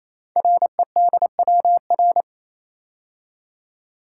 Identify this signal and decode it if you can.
Morse code, heard as a keyed tone, about 27 wpm, 700 Hz. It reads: REDWR
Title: Code Groups - 5 Letters - 28wpm